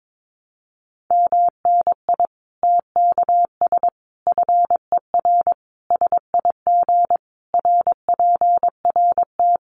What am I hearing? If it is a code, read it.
Morse code, heard as a keyed tone, about 22 wpm, 700 Hz. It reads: MNI TKS FER SIG RPRT
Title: Common QSO Phrases 22wpm